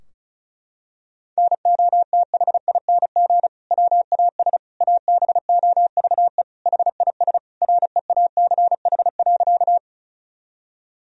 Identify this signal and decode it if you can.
Morse code, heard as a keyed tone, about 36 wpm, 700 Hz. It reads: NOTHING WAS ABOVE HIS REACH.
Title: Sentences from Top 1000 Words - Review Part 2 - 35wpm